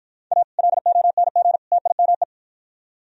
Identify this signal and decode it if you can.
Morse code, heard as a keyed tone, about 52 wpm, 700 Hz. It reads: A LONG TIME